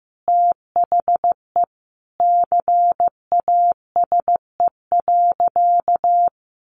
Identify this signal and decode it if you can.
Morse code, heard as a keyed tone, about 15 wpm, 700 Hz. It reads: THE CASE.
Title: Sentences from Top 1400 Words 15wpm